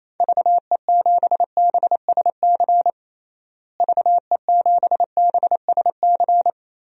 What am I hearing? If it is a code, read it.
Morse code, heard as a keyed tone, about 27 wpm, 700 Hz. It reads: VE7BSC VE7BSC
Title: DXCC-100 Call Signs - 6 Characters - Repeated 2x - 28wpm